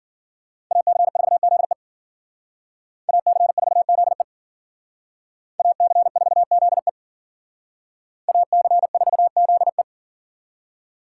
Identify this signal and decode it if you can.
Morse code, heard as a keyed tone, about 52 wpm, 700 Hz. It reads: AC4ZE AC4ZE AC4ZE AC4ZE